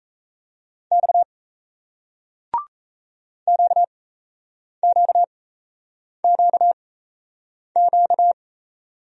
Morse code, heard as a keyed tone, about 38 wpm, spaced as for 8 wpm, 700 Hz. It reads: X QQQQ